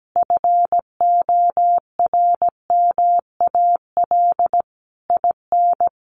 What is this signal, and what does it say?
Morse code, heard as a keyed tone, about 17 wpm, 700 Hz. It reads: FORMAL IN
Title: Sentences from Top 1300 Words 17wpm